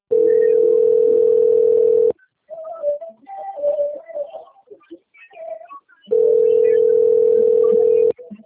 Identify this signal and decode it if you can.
Ringback tone (North America), 440 + 480 Hz.